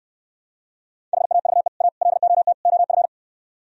Morse code, heard as a keyed tone, about 67 wpm, 700 Hz. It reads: HAVE A LOT OF